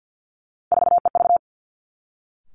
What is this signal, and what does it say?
Morse code, heard as a keyed tone, about 50 wpm, 700 Hz. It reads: 4EV
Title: Three Challenging Letter-Number Combinations 50wpm